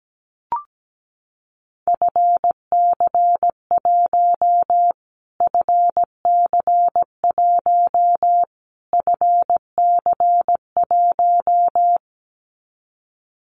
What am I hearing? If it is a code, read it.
Morse code, heard as a keyed tone, about 17 wpm, 700 Hz. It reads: FC1 FC1 FC1
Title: Callsign Prefixes for Top 25 DXCC - Repeated 3x - 17wpm